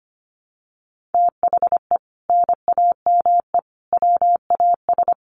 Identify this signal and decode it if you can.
Morse code, heard as a keyed tone, about 25 wpm, 700 Hz. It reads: THE NAME WAS
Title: Sentences from Top 1000 Words 25wpm